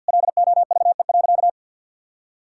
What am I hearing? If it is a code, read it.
Morse code, heard as a keyed tone, about 52 wpm, 700 Hz. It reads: LOVE.
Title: Sentences from Top 1000 Words - Review Part 5 - 50wpm